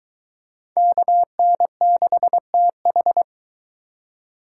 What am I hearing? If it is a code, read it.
Morse code, heard as a keyed tone, about 23 wpm, 700 Hz. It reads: KN6TH